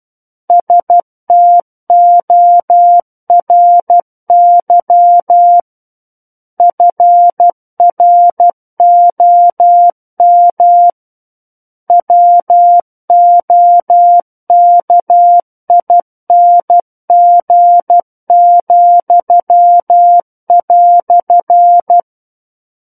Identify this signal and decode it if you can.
Morse code, heard as a keyed tone, about 12 wpm, 700 Hz. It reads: STORY FROM WOKING,"
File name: Book One / 03-Chapter 3 - 12 WPM